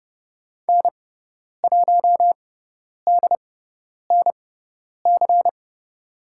Morse code, heard as a keyed tone, about 30 wpm, spaced as for 10 wpm, 700 Hz. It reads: N1DNC
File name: Lesson 16g - Letter c - Call Signs 30wpm